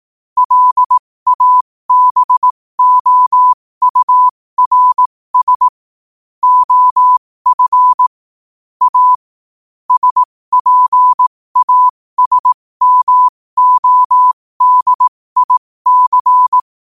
Morse code, heard as a keyed tone, about 18 wpm, spaced as for 15 wpm, 1000 Hz. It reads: LABOURS OF A SPASMODIC
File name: qotdfiles / 15wpm